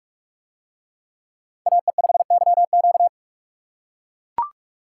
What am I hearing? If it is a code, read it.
Morse code, heard as a keyed tone, about 43 wpm, 700 Hz. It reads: AE5YQ